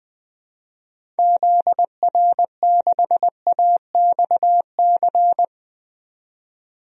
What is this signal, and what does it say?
Morse code, heard as a keyed tone, about 21 wpm, 700 Hz. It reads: ZR6AXC
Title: DXCC-100 Call Signs - 6 Characters - 20wpm